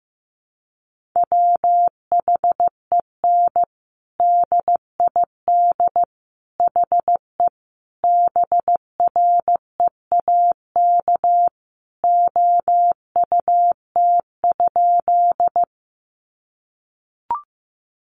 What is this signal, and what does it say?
Morse code, heard as a keyed tone, about 15 wpm, 700 Hz. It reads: WHEN DID HE BREAK OUT?
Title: Sentences from Top 900 Words 15wpm